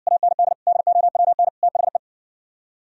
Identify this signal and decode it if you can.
Morse code, heard as a keyed tone, about 69 wpm, 700 Hz. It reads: AND DOWN THE